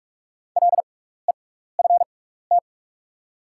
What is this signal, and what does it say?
Morse code, heard as a keyed tone, about 46 wpm, spaced as for 16 wpm, 700 Hz. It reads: LEFT
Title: 6x Character Spacing - Top 300 Words - 45wpm